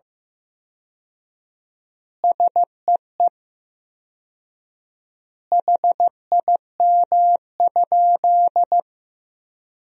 Morse code, heard as a keyed tone, about 15 wpm, 700 Hz. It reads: SEE HIM?
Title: 4x Word-Spacing - Sentences from Top 100 Words - 15wpm